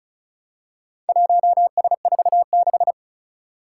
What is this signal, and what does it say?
Morse code, heard as a keyed tone, about 36 wpm, 700 Hz. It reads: 1S46
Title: Four Challenging Letter-Number Combinations 35wpm